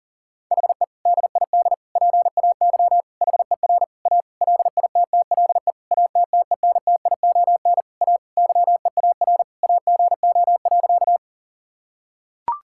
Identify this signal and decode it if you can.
Morse code, heard as a keyed tone, about 40 wpm, 700 Hz. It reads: HE DID PAY HER A LITTLE ATTENTION A YEAR AGO.
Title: Sentences from Top 1000 Words - Review Part 6 - 40wpm